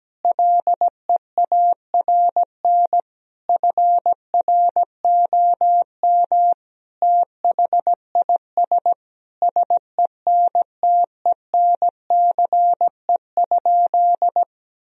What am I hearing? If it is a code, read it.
Morse code, heard as a keyed tone, about 17 wpm, 700 Hz. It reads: LEARN FROM THIS SENTENCE?
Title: Sentences from Top 1000 Words - Review Part 1 - 17wpm